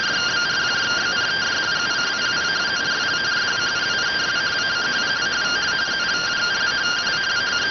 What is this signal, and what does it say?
Two-tone teleprinter signal, about 170 Hz shift.